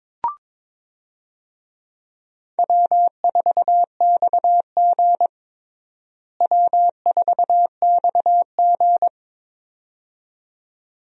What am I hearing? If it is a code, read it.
Morse code, heard as a keyed tone, about 22 wpm, 700 Hz. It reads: W4XG W4XG
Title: USA Call Signs - 4 Characters - Repeated 2x - 22wpm